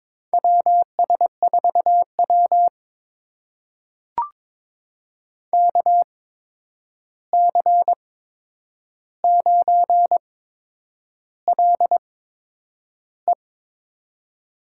Morse code, heard as a keyed tone, about 22 wpm, spaced as for 9 wpm, 700 Hz. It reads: WS4W KC9LE
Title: USA Call Signs - 8x Character Spacing - 22wpm